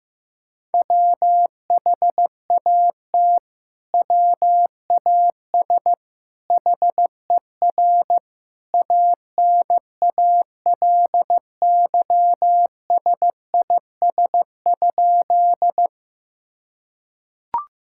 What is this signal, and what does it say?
Morse code, heard as a keyed tone, about 16 wpm, 700 Hz. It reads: WHAT WAS HER ANALYSIS?
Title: Sentences from Top 1000 Words - Review Part 5 - 15wpm